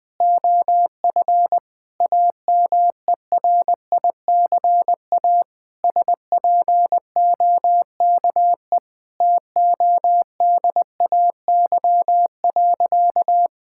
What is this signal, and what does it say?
Morse code, heard as a keyed tone, about 21 wpm, 700 Hz. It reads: OF AMERICA SPOKE TODAY.
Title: Sentences with 8-Word Phrases - Spoken Beginning - 20wpm